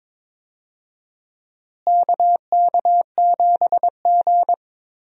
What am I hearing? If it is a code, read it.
Morse code, heard as a keyed tone, about 22 wpm, 700 Hz. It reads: KK7G
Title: USA Call Signs - 4 Characters - Repeated 3x - 22wpm